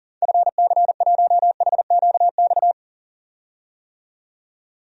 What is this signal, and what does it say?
Morse code, heard as a keyed tone, about 40 wpm, 700 Hz. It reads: FC1HQX
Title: DXCC-100 Call Signs - 6 Characters - 40wpm